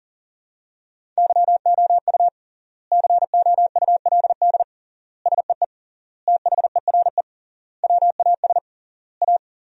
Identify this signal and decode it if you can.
Morse code, heard as a keyed tone, about 37 wpm, 700 Hz. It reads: YOU COULD SEE THERE WAS A
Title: 3x Word-Spacing - Sentences from Top 500 Words - 40wpm